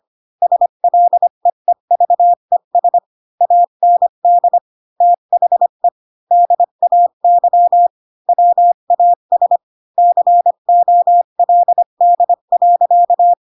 Morse code, heard as a keyed tone, about 25 wpm, 700 Hz. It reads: SLEEVES AND THE DAY WAS COLD.
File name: Pinocchio / Pinocchio ch8 20 25 ik1hge